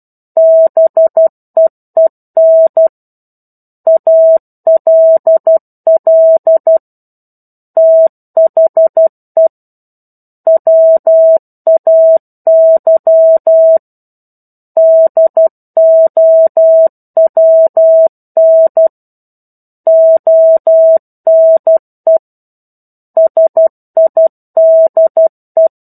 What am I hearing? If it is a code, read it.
Morse code, heard as a keyed tone, about 12 wpm, 650 Hz. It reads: BEEN ALL THE WAY DOWN ONE SIDE